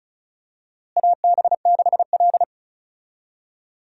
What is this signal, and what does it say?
Morse code, heard as a keyed tone, about 36 wpm, 700 Hz. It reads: AB6L
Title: USA Call Signs - 4 Characters - Repeated 3x - 35wpm